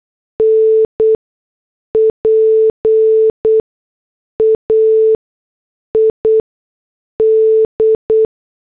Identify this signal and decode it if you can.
Morse code, heard as a keyed tone, about 8 wpm, spaced as for 6 wpm, 440 Hz. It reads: NPAID
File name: morse / headlines